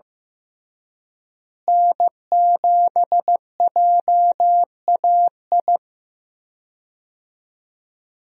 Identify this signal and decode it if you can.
Morse code, heard as a keyed tone, about 15 wpm, 700 Hz. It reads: N7JAI
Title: USA Call Signs - 5 Characters - Repeated 3x - 15wpm